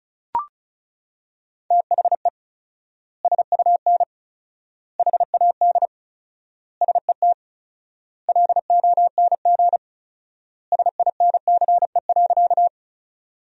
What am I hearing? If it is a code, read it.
Morse code, heard as a keyed tone, about 36 wpm, 700 Hz. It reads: THE SUN HAD SET LONG SINCE.
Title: 4x Word-Spacing - Sentences from Top 300 Words - 35wpm